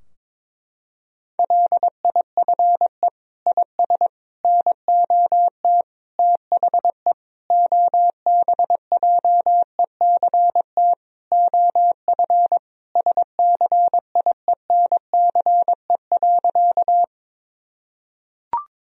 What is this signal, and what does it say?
Morse code, heard as a keyed tone, about 22 wpm, 700 Hz. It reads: LIFE IS NOT THE OBJECT OF SCIENCE.
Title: Sentences from Top 1000 Words - Review Part 3 - 22wpm